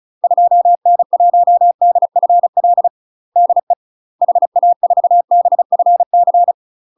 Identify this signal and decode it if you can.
Morse code, heard as a keyed tone, about 36 wpm, 700 Hz. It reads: 2N1DFL DE HA4BFC
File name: UK MASTER QSOs TRAINING / 35WPM / UK MASTER QSOs 0033